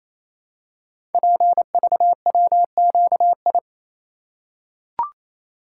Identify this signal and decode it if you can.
Morse code, heard as a keyed tone, about 27 wpm, 700 Hz. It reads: PVWQI
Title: Code Groups - 5 Letters - 28wpm